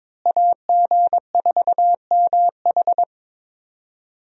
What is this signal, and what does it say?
Morse code, heard as a keyed tone, about 21 wpm, 700 Hz. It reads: AG4MH
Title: USA Call Signs - 5 Characters - Repeated 3x - 22wpm